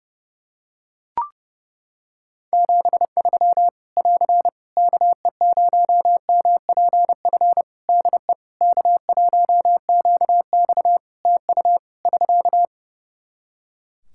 Morse code, heard as a keyed tone, about 30 wpm, 700 Hz. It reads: 73 <AR> KE0MPF DE K1QX TU <SK>
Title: Practice QSO - Exchange 4 - Ending - 30wpm